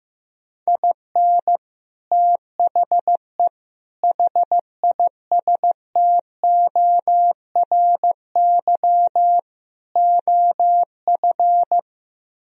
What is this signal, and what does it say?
Morse code, heard as a keyed tone, about 15 wpm, 700 Hz. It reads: IN THE HISTORY OF
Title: Speed-Racing - Sets of 4 Words 15wpm